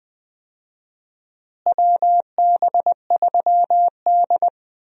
Morse code, heard as a keyed tone, about 20 wpm, 700 Hz. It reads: WB3D
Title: USA Call Signs - 4 Characters - Repeated 3x - 20wpm